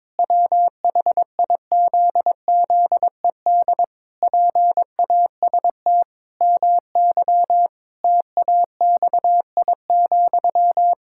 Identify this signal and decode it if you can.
Morse code, heard as a keyed tone, about 22 wpm, 700 Hz. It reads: WHIZZED PAST MY TAXI,
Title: Pangram Challenge v2 - 22wpm